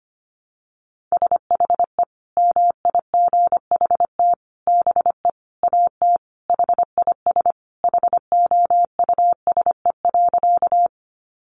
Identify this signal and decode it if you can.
Morse code, heard as a keyed tone, about 25 wpm, 700 Hz. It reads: SHE MIGHT BE AT HIS HOUSE.
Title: Sentences from Top 300 Words 25wpm